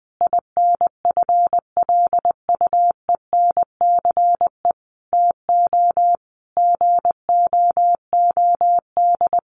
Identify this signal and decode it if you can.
Morse code, heard as a keyed tone, about 20 wpm, 700 Hz. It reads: INFLUENCE TO GOOD
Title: Sentences from Top 1100 Words 20wpm